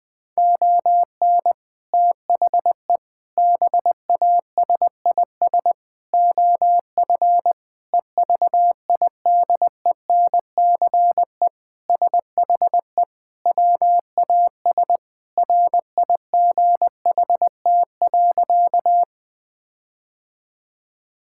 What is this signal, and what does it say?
Morse code, heard as a keyed tone, about 20 wpm, 700 Hz. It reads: ON THE BASIS OF EVIDENCE SHE WAS RIGHT.
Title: Sentences with 4-Word Phrases - Spoken Beginning - 20wpm